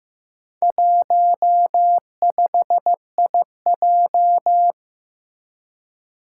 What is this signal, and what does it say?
Morse code, heard as a keyed tone, about 15 wpm, 700 Hz. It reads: 15IJ
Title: Four Challenging Letter-Number Combinations 15wpm